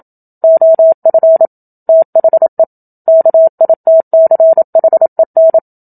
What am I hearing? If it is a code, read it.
Morse code, heard as a keyed tone, about 27 wpm, 650 Hz. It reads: OF THE KITCHEN
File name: Book Two / 01-Chapter 1 - 27 WPM